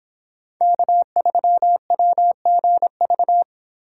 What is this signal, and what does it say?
Morse code, heard as a keyed tone, about 31 wpm, 700 Hz. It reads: K3WGV